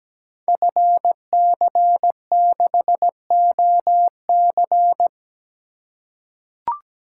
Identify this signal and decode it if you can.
Morse code, heard as a keyed tone, about 17 wpm, 700 Hz. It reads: FC6OC
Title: Code Groups - 5 Letter:Numbers - 17wpm